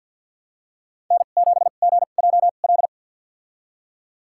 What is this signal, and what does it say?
Morse code, heard as a keyed tone, about 46 wpm, spaced as for 35 wpm, 690 Hz. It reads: N7GJL
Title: USA Call Signs - 5 Characters - 2x Character Spacing - 50wpm